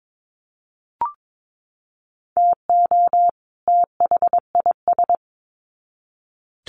Morse code, heard as a keyed tone, about 22 wpm, 700 Hz. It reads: TO THIS
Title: Sets of 2 Words 22wpm